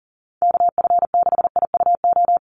Morse code, heard as a keyed tone, about 40 wpm, 700 Hz. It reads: KF6IUO
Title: USA Call Signs 40wpm